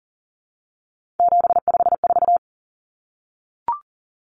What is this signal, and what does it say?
Morse code, heard as a keyed tone, about 40 wpm, 700 Hz. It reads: 754
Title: Common Cut Numbers - 3-Digit 40wpm